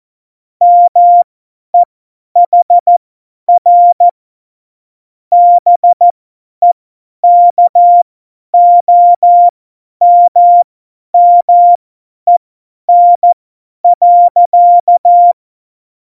Morse code, heard as a keyed tone, about 14 wpm, spaced as for 10 wpm, 700 Hz. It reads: MEHR BEKOMMEN.
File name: Morsekurs / Witze / Witz 21. November 2017, 50 F70